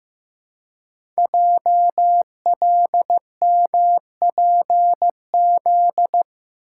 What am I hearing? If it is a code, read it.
Morse code, heard as a keyed tone, about 15 wpm, 690 Hz. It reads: JLMPZ
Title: Code Groups - 5 Letters - 15wpm